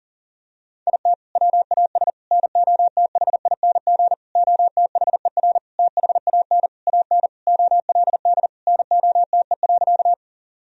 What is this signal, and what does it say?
Morse code, heard as a keyed tone, about 40 wpm, 700 Hz. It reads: IT WAS NOTHING OTHER THAN AN OLD NOTE.
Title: Sentences from Top 1000 Words - Review Part 2 - 40wpm